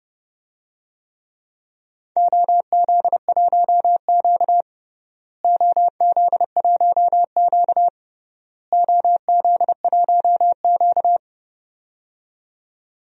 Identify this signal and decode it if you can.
Morse code, heard as a keyed tone, about 30 wpm, 700 Hz. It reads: OZ1Q OZ1Q OZ1Q
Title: DXCC-100 Call Signs - 4 Characters - Repeated 3x - 30wpm